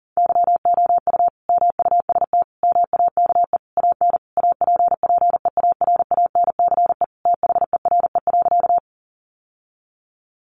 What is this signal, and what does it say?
Morse code, heard as a keyed tone, about 40 wpm, 700 Hz. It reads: YOU MUST MAKE AN APPEARANCE THERE.
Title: Sentences from Top 1100 Words 40wpm